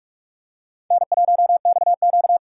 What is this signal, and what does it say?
Morse code, heard as a keyed tone, about 41 wpm, 700 Hz. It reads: N1XQ
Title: USA Call Signs - 4 Characters - 2x Character Spacing - 45wpm